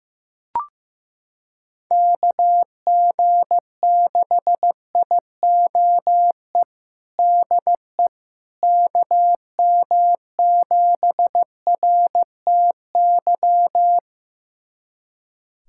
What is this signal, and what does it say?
Morse code, heard as a keyed tone, about 15 wpm, 700 Hz. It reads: KG6IOE DE KM7RTY
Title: USA Callsign DE Callsign 15wpm